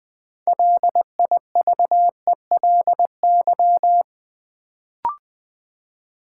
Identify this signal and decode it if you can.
Morse code, heard as a keyed tone, about 20 wpm, 700 Hz. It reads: LIVELY
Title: Build A Word - 3 Letters At A Time - 6-Letter Words - 20wpm